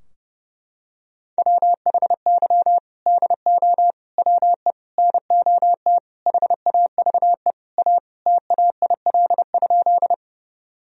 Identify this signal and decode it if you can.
Morse code, heard as a keyed tone, about 32 wpm, 700 Hz. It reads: WHY DO WE NOT HAVE A TAIL?
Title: Sentences from Top 1000 Words - Review Part 3 - 30wpm